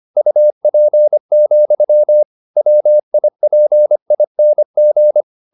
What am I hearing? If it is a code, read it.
Morse code, heard as a keyed tone, about 25 wpm, 600 Hz. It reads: UP, WIPING
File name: the old man and the sea part 1 25wpm